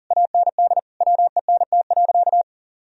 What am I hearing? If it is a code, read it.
Morse code, heard as a keyed tone, about 37 wpm, 700 Hz. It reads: AND WENT.
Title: Sentences from Top 1000 Words - Review Part 2 - 40wpm